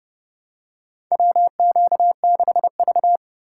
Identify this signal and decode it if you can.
Morse code, heard as a keyed tone, about 30 wpm, 700 Hz. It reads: WQ6V
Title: USA Call Signs - 4 Characters - 4x Character Spacing - 30wpm